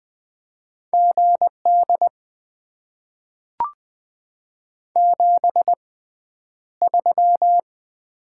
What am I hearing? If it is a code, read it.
Morse code, heard as a keyed tone, about 20 wpm, spaced as for 10 wpm, 700 Hz. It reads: GD 73
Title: Top 90 QSO Elements - 6x Character Spacing - 20wpm